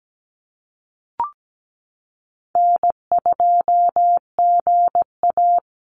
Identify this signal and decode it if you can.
Morse code, heard as a keyed tone, about 17 wpm, 700 Hz. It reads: N2GA
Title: Easy Call Signs - 17wpm